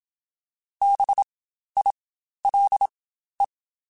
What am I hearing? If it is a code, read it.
Morse code, heard as a keyed tone, about 27 wpm, spaced as for 12 wpm, 800 Hz. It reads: BILE
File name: Spaced Alphas / Spaced Alphas mp3s / Bs sp @25